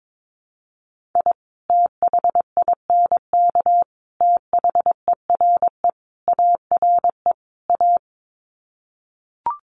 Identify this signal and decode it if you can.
Morse code, heard as a keyed tone, about 22 wpm, 700 Hz. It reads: I THINK THERE ARE A
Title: Sets of 5 Words Encore2 22wpm